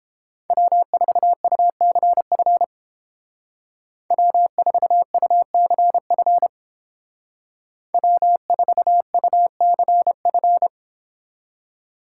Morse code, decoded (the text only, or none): W4UCF W4UCF W4UCF